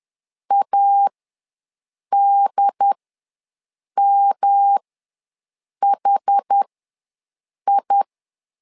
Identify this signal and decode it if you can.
Morse code, heard as a keyed tone, about 10 wpm, spaced as for 6 wpm, 800 Hz. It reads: ADMHI
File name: LEZIONI / 3 MWRO / 3 M ws 10wpm 2121